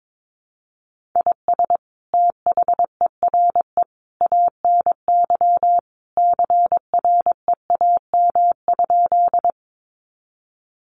Morse code, heard as a keyed tone, about 22 wpm, 700 Hz. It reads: IS THERE ANY CREAM?
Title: Sentences from Top 1500 Words 22wpm